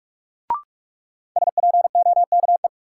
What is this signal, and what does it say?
Morse code, heard as a keyed tone, about 46 wpm, 700 Hz. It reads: SPOKE
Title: Sentences from Top 1200 Words - New Word Drill - 45wpm